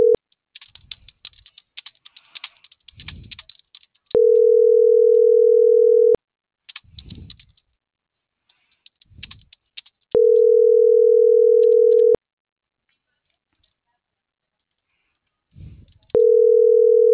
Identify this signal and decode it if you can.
Ringback tone (North America), 440 + 480 Hz.